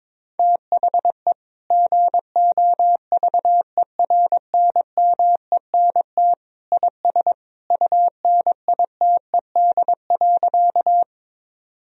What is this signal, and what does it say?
Morse code, heard as a keyed tone, about 22 wpm, 700 Hz. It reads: THE GOVERNMENT IS UNITED.
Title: Sentences from Top 1000 Words - Review Part 4 - 22wpm